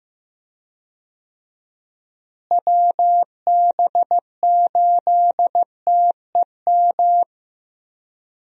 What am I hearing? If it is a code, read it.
Morse code, heard as a keyed tone, about 15 wpm, 700 Hz. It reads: WB8TEM
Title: USA Call Signs - 6 Characters - Repeated 3x - 15wpm